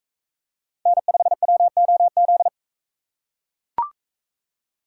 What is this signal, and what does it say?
Morse code, heard as a keyed tone, about 40 wpm, 700 Hz. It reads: N5WOZ